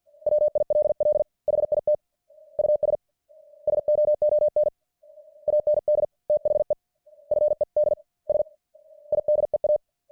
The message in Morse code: WILL HIT US SOON AND THE FEDS IDEA